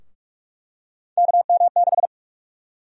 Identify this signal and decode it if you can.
Morse code, heard as a keyed tone, about 43 wpm, 700 Hz. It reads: KM6
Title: USA Call Sign Prefixes - Repeated 2x - 45wpm